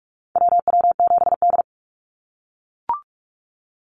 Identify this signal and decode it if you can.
Morse code, heard as a keyed tone, about 44 wpm, 700 Hz. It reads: WW7D
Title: NA SOTA Activator Callsigns 45wpm